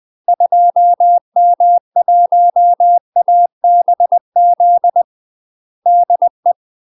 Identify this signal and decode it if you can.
Morse code, heard as a keyed tone, about 20 wpm, 700 Hz. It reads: 2M1ABZ DE
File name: UK MASTER QSOs TRAINING / 20WPM / UK MASTER QSOs 0065